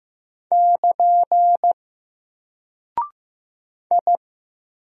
Morse code, heard as a keyed tone, about 15 wpm, 700 Hz. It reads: <KN> I
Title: Single Character (Letter:Number:Punctuation:Prosign) Alt 15wpm